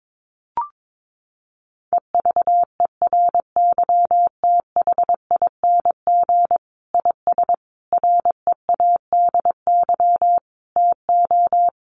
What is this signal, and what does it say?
Morse code, heard as a keyed tone, about 22 wpm, 700 Hz. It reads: EVERYTHING IS READY TO
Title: Sentences from Top 700 Words 22wpm